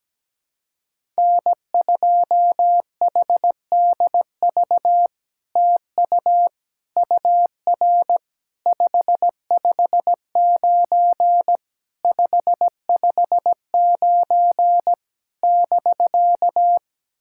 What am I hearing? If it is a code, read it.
Morse code, heard as a keyed tone, about 17 wpm, 700 Hz. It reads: N2HDV TU UR 559 559 <BK>
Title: SOTA QSO Practice 17wpm